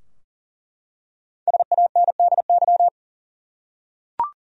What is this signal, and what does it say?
Morse code, heard as a keyed tone, about 40 wpm, 700 Hz. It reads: SANDY
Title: USA Common Names - Max 5 Letters - 40wpm